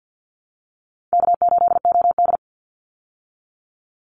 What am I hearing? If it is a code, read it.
Morse code, heard as a keyed tone, about 48 wpm, 700 Hz. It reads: K8OD
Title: Easy Call Signs Repeated 2x - 50wpm